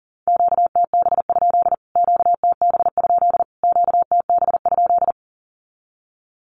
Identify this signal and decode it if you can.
Morse code, heard as a keyed tone, about 40 wpm, 700 Hz. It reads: QTB? QTB? QTB?
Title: Q-Codes Repeated 3x 40wpm